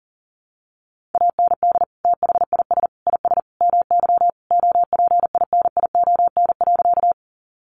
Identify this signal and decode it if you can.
Morse code, heard as a keyed tone, about 40 wpm, 700 Hz. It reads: AND THIS IS MY OPINION.
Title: Sentences from Top 1200 Words 40wpm